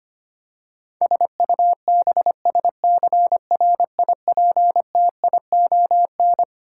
Morse code, heard as a keyed tone, about 25 wpm, 700 Hz. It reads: SUBSCRIPTION
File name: Twelve Letter Words - 25wpm